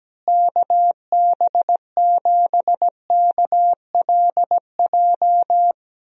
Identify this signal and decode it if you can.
Morse code, heard as a keyed tone, about 17 wpm, 700 Hz. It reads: KB7KLJ
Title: USA Call Signs - 6 Characters - Repeated 2x - 17wpm